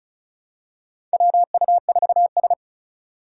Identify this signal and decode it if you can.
Morse code, heard as a keyed tone, about 34 wpm, 700 Hz. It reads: WU4S
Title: USA Call Signs - 2x Character Spacing - 35wpm